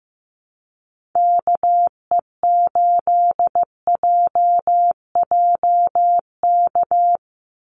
Morse code, heard as a keyed tone, about 15 wpm, 700 Hz. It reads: KE8JJK
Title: USA Call Signs 15wpm